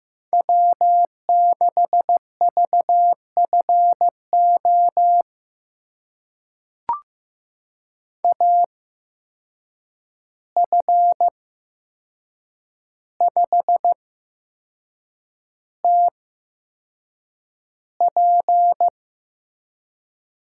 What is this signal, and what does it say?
Morse code, heard as a keyed tone, about 15 wpm, spaced as for 6 wpm, 700 Hz. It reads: W6VFO AF5TP
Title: USA Call Signs - 5 Characters - 8x Character Spacing - 15wpm